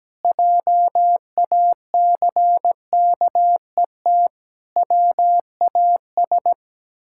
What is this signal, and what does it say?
Morse code, heard as a keyed tone, about 17 wpm, 700 Hz. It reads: JACKET WAS
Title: Pangram Challenge v2 - 17wpm